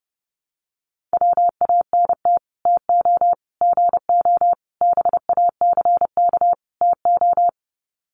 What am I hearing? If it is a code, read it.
Morse code, heard as a keyed tone, about 30 wpm, 700 Hz. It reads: WANT TO GO BACK TO
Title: Sets of 5 Words 30wpm